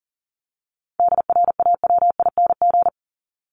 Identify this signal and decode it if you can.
Morse code, heard as a keyed tone, about 40 wpm, 700 Hz. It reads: DRAWING
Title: ING Suffix - Top 100 Words with Base Words 40wpm